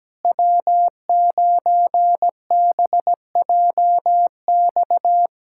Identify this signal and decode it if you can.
Morse code, heard as a keyed tone, about 17 wpm, 700 Hz. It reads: W9BJX
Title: USA Callsign DE Callsign - 5 Characters - 17wpm